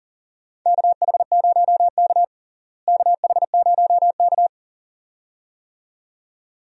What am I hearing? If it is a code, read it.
Morse code, heard as a keyed tone, about 40 wpm, 700 Hz. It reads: KH0K KH0K
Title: USA Call Signs - 4 Characters - Repeated 2x - 40wpm